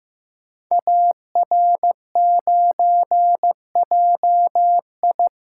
Morse code, heard as a keyed tone, about 15 wpm, 700 Hz. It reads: AR9JI
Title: Code Groups - 5 Letter:Numbers - 15wpm